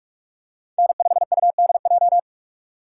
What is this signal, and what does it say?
Morse code, heard as a keyed tone, about 48 wpm, 700 Hz. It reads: N5UDJ
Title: USA Call Signs - 5 Characters - 2x Character Spacing - 45wpm